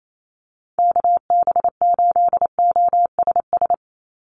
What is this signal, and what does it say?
Morse code, heard as a keyed tone, about 28 wpm, 700 Hz. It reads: KB8OSS
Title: USA Call Signs 28wpm